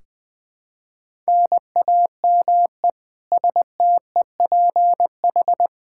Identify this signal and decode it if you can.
Morse code, heard as a keyed tone, about 21 wpm, 700 Hz. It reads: NAME STEPH
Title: Practice QSO - Exchange 1 - Beginning - 20wpm